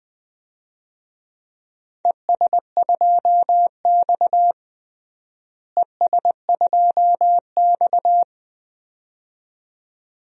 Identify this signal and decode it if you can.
Morse code, heard as a keyed tone, about 20 wpm, 700 Hz. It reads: ES2X ES2X
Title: DXCC-100 Call Signs - 4 Characters - Repeated 2x - 20wpm